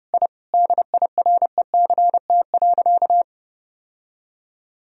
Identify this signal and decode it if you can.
Morse code, heard as a keyed tone, about 30 wpm, 700 Hz. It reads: I DIRECT.
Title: Sentences from Top 1000 Words - Review Part 2 - 30wpm